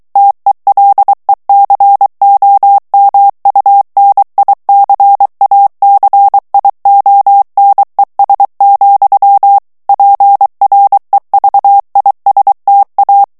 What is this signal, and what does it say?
Morse code, heard as a keyed tone, about 23 wpm, 800 Hz. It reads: TELECOMUNICACIONES, PREVISTA
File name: morse mp3 / ESPAGNOL TEXTE EN CLAIR 2 VITESSE 22 MOTS MINUTES